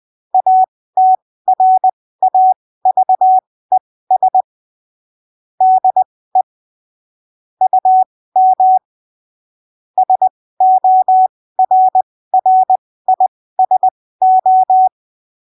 Morse code, heard as a keyed tone, about 20 wpm, spaced as for 15 wpm, 750 Hz. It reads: ATRAVES DE UM SORRISO